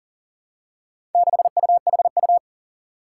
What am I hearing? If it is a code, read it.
Morse code, heard as a keyed tone, about 37 wpm, 700 Hz. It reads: 6UHU
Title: Four Challenging Letter-Number Combinations 40wpm